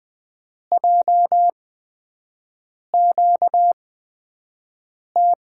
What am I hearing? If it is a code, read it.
Morse code, heard as a keyed tone, about 20 wpm, spaced as for 5 wpm, 700 Hz. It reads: JQT
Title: USA Call Signs - 6 Characters - 8x Character Spacing - 20wpm